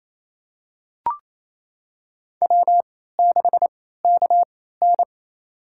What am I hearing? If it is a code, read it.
Morse code, heard as a keyed tone, about 28 wpm, spaced as for 16 wpm, 700 Hz. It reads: W6KN
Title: USA Call Signs - 3x Character Spacing - 28wpm